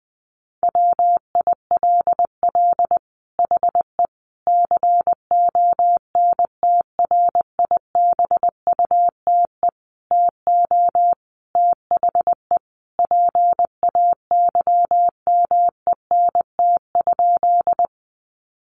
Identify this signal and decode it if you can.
Morse code, heard as a keyed tone, about 20 wpm, 700 Hz. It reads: WILL HE CONTRIBUTE TO THE PAYMENT?
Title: Sentences from Top 1400 Words 20wpm